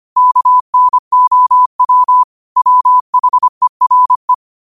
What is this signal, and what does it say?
Morse code, heard as a keyed tone, about 25 wpm, 1000 Hz. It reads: KNOW WHERE
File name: qotdfiles / 25wpm